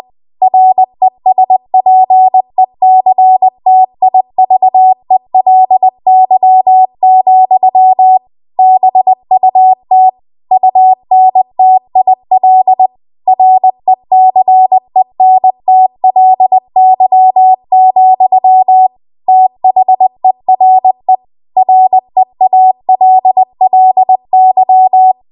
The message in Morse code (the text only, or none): RESPECTIVELY, BUT UNTIL RECENTLY, THERE REALLY